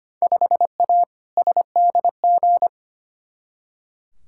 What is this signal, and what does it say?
Morse code, heard as a keyed tone, about 25 wpm, 700 Hz. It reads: 5A SDG
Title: Field Day Exchanges Encore - 25wpm